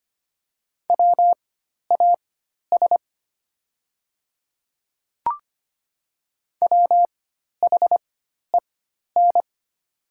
Morse code, heard as a keyed tone, about 25 wpm, spaced as for 12 wpm, 700 Hz. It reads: WAS WHEN
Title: 4x Character Spacing - Top 100 Words - 25wpm